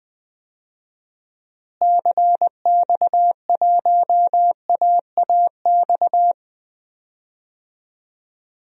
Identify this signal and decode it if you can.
Morse code, heard as a keyed tone, about 20 wpm, 700 Hz. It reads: CX1AAX
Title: DXCC-100 Call Signs - 6 Characters - Repeated 3x - 20wpm